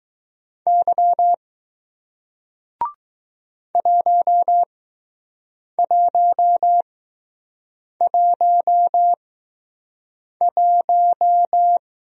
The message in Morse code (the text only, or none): Y 1111